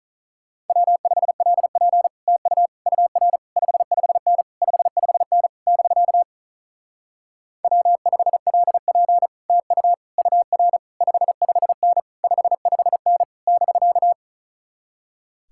Morse code, heard as a keyed tone, about 44 wpm, 700 Hz. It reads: W5LP TU UR 55N 55N <BK> W5LP TU UR 55N 55N <BK>